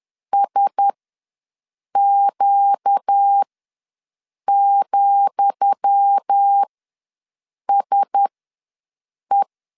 Morse code, heard as a keyed tone, about 10 wpm, spaced as for 6 wpm, 800 Hz. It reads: SQ,SE
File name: LEZIONI / 10 SIGNS / 10D COMA ws 10wpm 4314